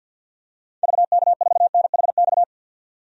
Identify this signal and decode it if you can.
Morse code, heard as a keyed tone, about 50 wpm, 700 Hz. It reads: VK4NHX
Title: DXCC-100 Call Signs - 6 Characters - 50wpm